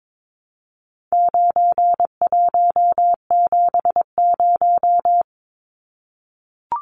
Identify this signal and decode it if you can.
Morse code, heard as a keyed tone, about 22 wpm, 700 Hz. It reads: 9170
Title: Numbers - 4-Digit 22wpm